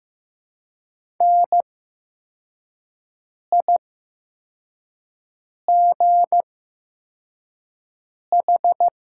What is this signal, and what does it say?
Morse code, heard as a keyed tone, about 15 wpm, spaced as for 4 wpm, 700 Hz. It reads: NIGH
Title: 8x Character Spacing - Top 300 Words - 15wpm